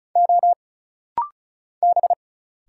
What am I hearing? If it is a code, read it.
Morse code, heard as a keyed tone, about 36 wpm, 700 Hz. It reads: O B
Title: Single Letters - Rapid-Fire 35wpm